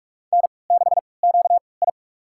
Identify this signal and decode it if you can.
Morse code, heard as a keyed tone, about 42 wpm, spaced as for 25 wpm, 700 Hz. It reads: N6QI
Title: USA Call Signs - 4 Characters - 3x Character Spacing - 45wpm